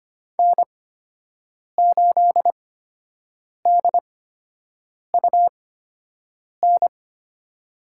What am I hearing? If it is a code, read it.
Morse code, heard as a keyed tone, about 25 wpm, spaced as for 7 wpm, 700 Hz. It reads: N8DUN
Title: USA Call Signs - 5 Characters - 8x Character Spacing - 25wpm